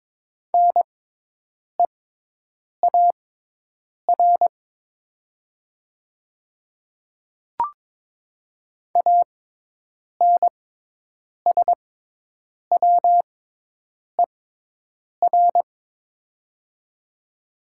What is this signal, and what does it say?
Morse code, heard as a keyed tone, about 22 wpm, spaced as for 8 wpm, 700 Hz. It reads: NEAR ANSWER
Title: 6x Character Spacing - Top 300 Words - 22wpm